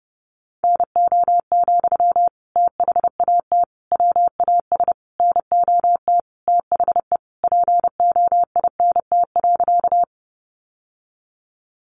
Morse code, heard as a keyed tone, about 28 wpm, 700 Hz. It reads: NO, THAT WAS NOT THE POINT.
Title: Sentences from Top 600 Words 30wpm